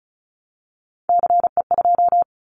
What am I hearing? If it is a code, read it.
Morse code, heard as a keyed tone, about 34 wpm, 700 Hz. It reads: CE2
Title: Callsign Prefixes for Top 100 DXCC - 35wpm